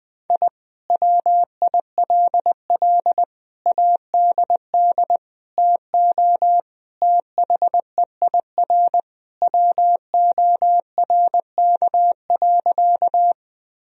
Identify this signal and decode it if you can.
Morse code, heard as a keyed tone, about 21 wpm, 700 Hz. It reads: I WILL ADD TO THEIR WORK.
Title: Sentences from Top 1000 Words - Review Part 1 - 20wpm